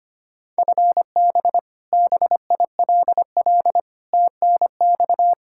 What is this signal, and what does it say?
Morse code, heard as a keyed tone, about 25 wpm, 700 Hz. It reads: FB BILL TNX
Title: Practice QSO - Exchange 3 - More Info - 25wpm